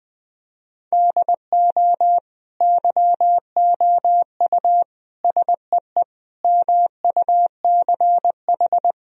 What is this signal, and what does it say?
Morse code, heard as a keyed tone, about 20 wpm, 700 Hz. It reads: DO YOU SEE MUCH
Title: Sentences from Top 1000 Words - Review Part 4 - 20wpm